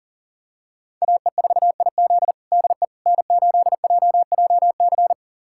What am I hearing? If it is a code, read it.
Morse code, heard as a keyed tone, about 37 wpm, 700 Hz. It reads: AE4IZ DE N8JJC
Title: USA Callsign DE Callsign - 5 Characters - 40wpm